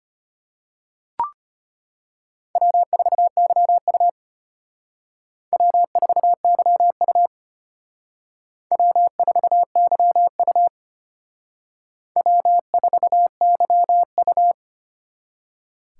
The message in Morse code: W4YU W4YU W4YU W4YU

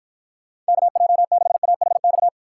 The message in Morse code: KJ6AHX

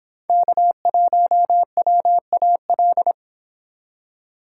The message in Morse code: K1WAL

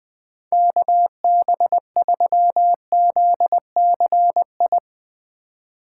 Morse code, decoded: KB3ZCI

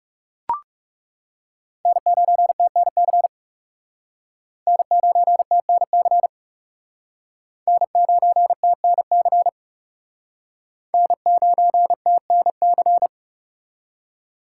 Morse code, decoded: N9TNC N9TNC N9TNC N9TNC

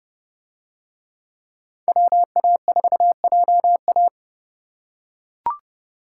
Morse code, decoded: WA4JA

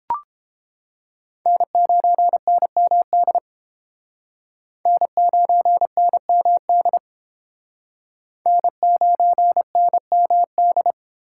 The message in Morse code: N9NMD N9NMD N9NMD

